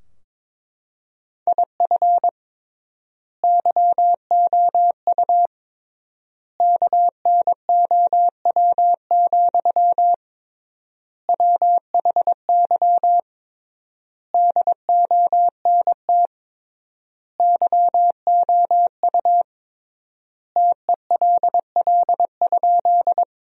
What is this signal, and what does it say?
Morse code, heard as a keyed tone, about 22 wpm, 700 Hz. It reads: IF YOU KNOW, WHY DONT YOU TELL?
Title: 3x Word-Spacing - Sentences from Top 300 Words - 22wpm